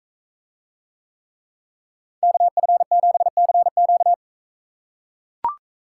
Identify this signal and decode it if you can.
Morse code, heard as a keyed tone, about 41 wpm, 700 Hz. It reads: KF7CQ